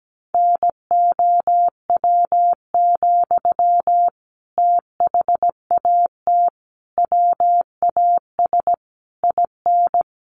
Morse code, heard as a keyed tone, about 17 wpm, 700 Hz. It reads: NOW, THAT WAS IN
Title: Sentences from Top 1200 Words 17wpm